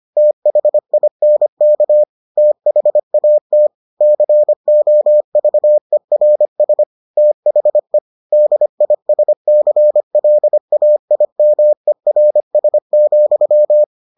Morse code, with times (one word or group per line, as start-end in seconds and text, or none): 0.2-2.0 THINK
2.4-3.7 THAT
4.0-6.8 COVERS
7.2-8.0 THE
8.3-13.8 DISCLAIMERS,